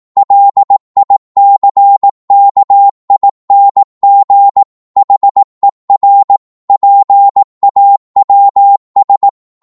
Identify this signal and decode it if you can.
Morse code, heard as a keyed tone, about 18 wpm, 800 Hz. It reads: LICKING HER PAWS
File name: Morse / psg-18wpm